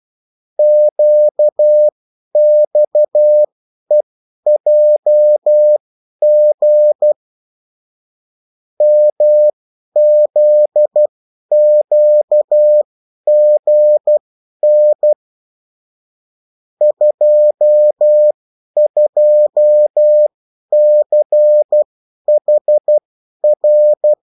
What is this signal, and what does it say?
Morse code, heard as a keyed tone, about 12 wpm, spaced as for 10 wpm, 600 Hz. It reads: QXEJG MZQGN 22CHR
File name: morse mp3 / revision9-vitesse 10 mots